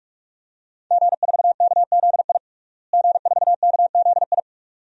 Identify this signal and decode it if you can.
Morse code, heard as a keyed tone, about 43 wpm, 700 Hz. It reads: G4KZI G4KZI
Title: DXCC-100 Call Signs - 5 Characters - Repeated 2x - 45wpm